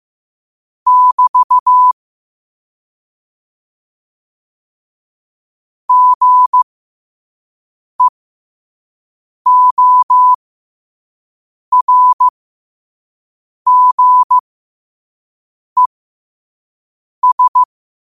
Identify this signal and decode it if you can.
Morse code, heard as a keyed tone, about 15 wpm, spaced as for 5 wpm, 1000 Hz. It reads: <BT> GEORGES
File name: qotdfiles / 5wpm